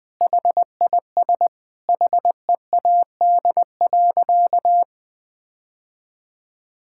Morse code, heard as a keyed tone, about 21 wpm, 700 Hz. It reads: HIS HEAD.
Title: Sentences with 2-Word Phrases - 20wpm